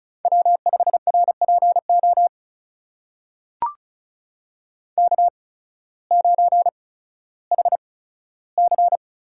Morse code, heard as a keyed tone, about 36 wpm, spaced as for 16 wpm, 690 Hz. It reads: W5RPO K9HC
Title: USA Call Signs - 5 Characters - 8x Character Spacing - 35wpm